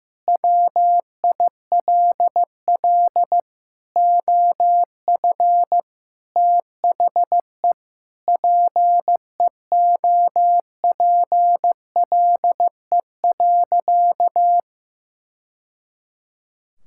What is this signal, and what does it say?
Morse code, heard as a keyed tone, about 16 wpm, 700 Hz. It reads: WILL OF THE PEOPLE.
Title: Sentences from Top 1000 Words - Review Part 5 - 15wpm